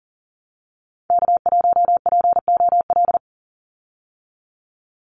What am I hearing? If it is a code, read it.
Morse code, heard as a keyed tone, about 42 wpm, 700 Hz. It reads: K1POL
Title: USA Call Signs 40wpm